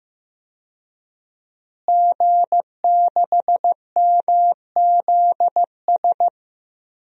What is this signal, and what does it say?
Morse code, heard as a keyed tone, about 15 wpm, 700 Hz. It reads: G6MZS
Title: DXCC-100 Call Signs - 5 Characters - Repeated 2x - 15wpm